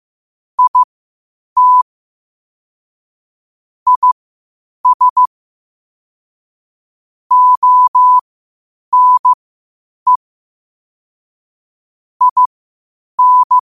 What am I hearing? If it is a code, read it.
Morse code, heard as a keyed tone, about 15 wpm, spaced as for 8 wpm, 1000 Hz. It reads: IT IS ONE IN
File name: qotdfiles / 8wpm